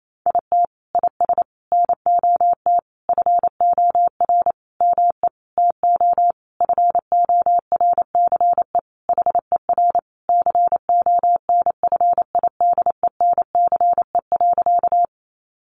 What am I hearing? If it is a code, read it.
Morse code, heard as a keyed tone, about 28 wpm, 700 Hz. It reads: IT IS NOT FOR ME TO FORCE HER CONFIDENCE.
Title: Sentences from Top 1500 Words 28wpm